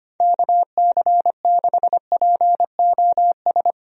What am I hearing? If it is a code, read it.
Morse code, heard as a keyed tone, about 25 wpm, 700 Hz. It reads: KC6POS
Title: USA Call Signs - 6 Characters - Repeated 2x - 25wpm